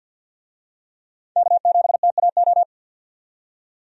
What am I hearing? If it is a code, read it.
Morse code, heard as a keyed tone, about 45 wpm, 700 Hz. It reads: K7TAO
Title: USA Call Signs - 5 Characters - 50wpm